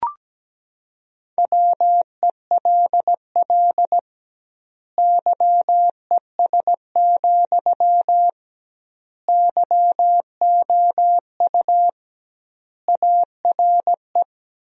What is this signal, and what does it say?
Morse code, heard as a keyed tone, about 17 wpm, 700 Hz. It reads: WELL YES, YOU ARE
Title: 2x Word-Spacing - Sentences from Top 500 Words - 17wpm